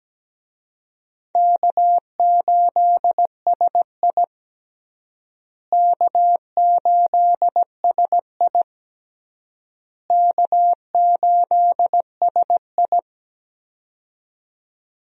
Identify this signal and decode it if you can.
Morse code, heard as a keyed tone, about 17 wpm, 700 Hz. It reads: K8SI K8SI K8SI
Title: USA Call Signs - 4 Characters - Repeated 3x - 17wpm